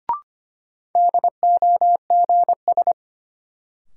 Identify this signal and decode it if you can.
Morse code, heard as a keyed tone, about 25 wpm, 700 Hz. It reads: DOGS
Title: Sentences from Top 1600 Words - New Word Drill - 25wpm